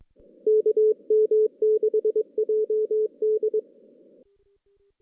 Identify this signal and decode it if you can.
Morse code, heard as a keyed tone, about 22 wpm, 430 Hz. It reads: KM6JD
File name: 2016 JAN CW NAQP / 2016 Jan naqp k5ot - 0-5